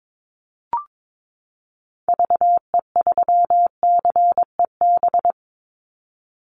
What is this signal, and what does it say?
Morse code, heard as a keyed tone, about 22 wpm, 700 Hz. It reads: VE3CEB
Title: Call Signs Repeated 2x - 22wpm